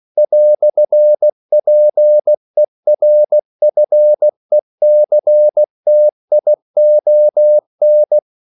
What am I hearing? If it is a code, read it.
Morse code, heard as a keyed tone, about 16 wpm, 600 Hz. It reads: "PERFECTION
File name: CW / Practice / EnglishQuotations0216